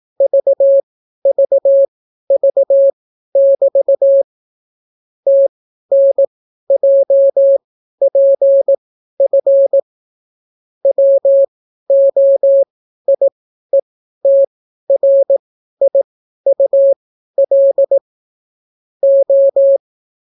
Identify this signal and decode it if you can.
Morse code, heard as a keyed tone, about 18 wpm, spaced as for 12 wpm, 550 Hz. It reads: VVV<BT> TNJPF WOIETRIUL O